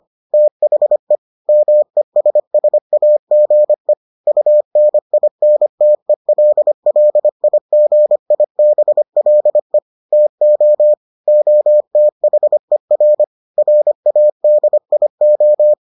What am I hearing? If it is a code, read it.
Morse code, heard as a keyed tone, about 25 wpm, 600 Hz. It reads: THE MESSAGE UNINTELLIGIBLE TO OTHER RADIO